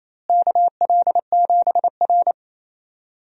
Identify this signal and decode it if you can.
Morse code, heard as a keyed tone, about 28 wpm, 700 Hz. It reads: KL7R
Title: DXCC-100 Call Signs - 4 Characters - Repeated 3x - 28wpm